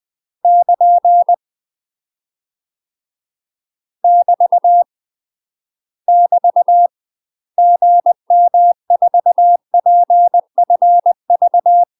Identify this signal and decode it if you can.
Morse code, heard as a keyed tone, about 20 wpm, 700 Hz. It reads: <KN> <BT> <BT> GM4PFV
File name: UK MASTER QSOs TRAINING / 20WPM / UK MASTER QSOs 0024